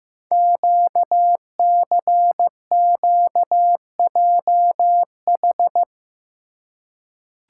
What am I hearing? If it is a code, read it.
Morse code, heard as a keyed tone, about 15 wpm, 700 Hz. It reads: QCQJH
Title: Code Groups - 5 Letters - 15wpm